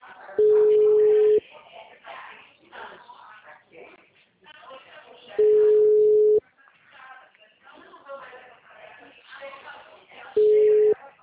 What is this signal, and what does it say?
Ringback tone (Europe, 425 Hz), with speech.